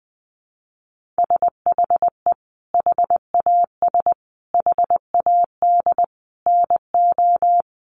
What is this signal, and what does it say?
Morse code, heard as a keyed tone, about 20 wpm, 700 Hz. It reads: SHE HAS HAD NO
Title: Sentences from Top 1400 Words 20wpm